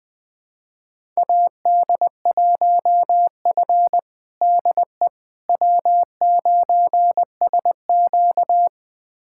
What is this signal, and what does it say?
Morse code, heard as a keyed tone, about 21 wpm, 700 Hz. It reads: AD1F DE W9SQ
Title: USA Callsign DE Callsign - 4 Characters - 20wpm